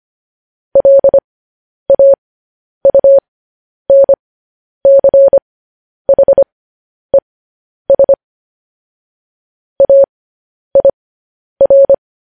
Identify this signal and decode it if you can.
Morse code, heard as a keyed tone, about 25 wpm, spaced as for 10 wpm, 550 Hz. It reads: LAUNCHES AIR
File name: podcast / News headlines-1x-25x10